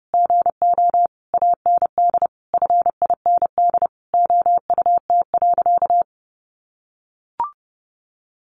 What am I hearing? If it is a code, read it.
Morse code, heard as a keyed tone, about 30 wpm, 700 Hz. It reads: GO AND FIND OUT.
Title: Sentences from Top 200 Words 30wpm